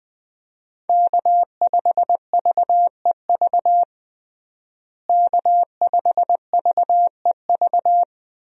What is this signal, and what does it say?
Morse code, heard as a keyed tone, about 20 wpm, 700 Hz. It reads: K5VEV K5VEV
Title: USA Call Signs - 5 Characters - Repeated 3x - 20wpm